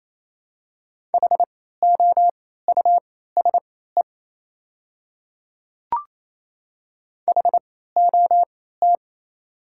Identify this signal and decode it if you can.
Morse code, heard as a keyed tone, about 28 wpm, spaced as for 16 wpm, 700 Hz. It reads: HOUSE HOT
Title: 3x Character Spacing - Top 300 Words - 28wpm